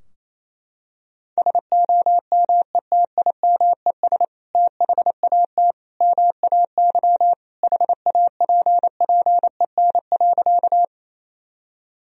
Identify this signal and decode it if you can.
Morse code, heard as a keyed tone, about 27 wpm, 690 Hz. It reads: SOMETIMES THAT MAY HAPPEN.
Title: Sentences from Top 1000 Words - Review Part 3 - 28wpm